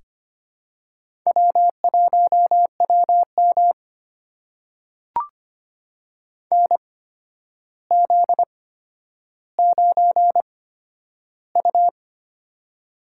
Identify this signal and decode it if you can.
Morse code, heard as a keyed tone, about 25 wpm, spaced as for 11 wpm, 700 Hz. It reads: W1WM NZ9U
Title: USA Call Signs - 4 Characters - 8x Character Spacing - 25wpm